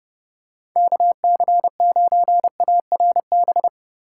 Morse code, heard as a keyed tone, about 30 wpm, 700 Hz. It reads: KC9ARB